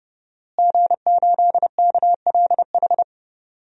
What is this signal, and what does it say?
Morse code, heard as a keyed tone, about 29 wpm, 700 Hz. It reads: G8KLH
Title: DXCC-100 Call Signs - 5 Characters - Repeated 3x - 30wpm